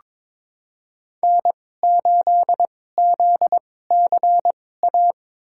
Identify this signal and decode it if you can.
Morse code, heard as a keyed tone, about 22 wpm, spaced as for 16 wpm, 700 Hz. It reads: N8ZCA
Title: USA Call Signs - 5 Characters - 2x Character Spacing - 22wpm